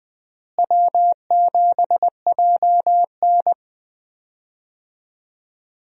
Morse code, heard as a keyed tone, about 20 wpm, 700 Hz. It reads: W7JN
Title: USA Call Signs - 4 Characters - Repeated 3x - 20wpm